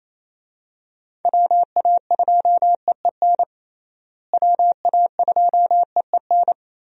Morse code, heard as a keyed tone, about 28 wpm, 700 Hz. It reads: WA2EEN WA2EEN
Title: USA Call Signs - 6 Characters - Repeated 3x - 28wpm